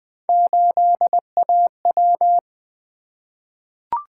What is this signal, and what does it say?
Morse code, heard as a keyed tone, about 20 wpm, 700 Hz. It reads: 8AW
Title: Three Letter-Number Combinations - 20wpm